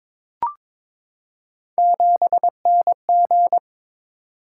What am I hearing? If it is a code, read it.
Morse code, heard as a keyed tone, about 22 wpm, 700 Hz. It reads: 7NG
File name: Three Letter-Number Combinations - 22wpm